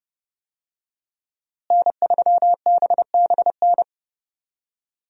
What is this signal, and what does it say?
Morse code, heard as a keyed tone, about 30 wpm, 700 Hz. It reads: N3BBN
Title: USA Call Signs - 5 Characters - Repeated 2x - 30wpm